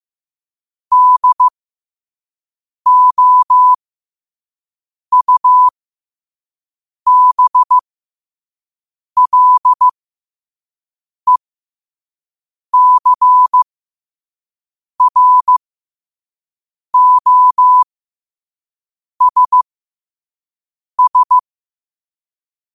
Morse code, heard as a keyed tone, about 15 wpm, spaced as for 5 wpm, 1000 Hz. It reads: DOUBLECROSS